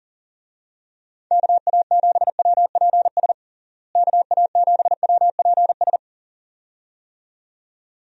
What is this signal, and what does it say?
Morse code, heard as a keyed tone, about 37 wpm, 700 Hz. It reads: KA7WPS KA7WPS
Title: USA Call Signs - 6 Characters - Repeated 2x - 40wpm